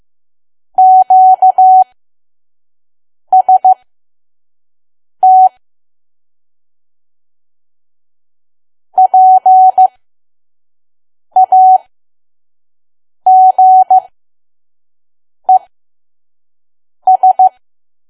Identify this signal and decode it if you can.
Morse code, heard as a keyed tone, about 15 wpm, spaced as for 5 wpm, 750 Hz. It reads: QST PAGES
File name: Archive / 5 WPM files / PM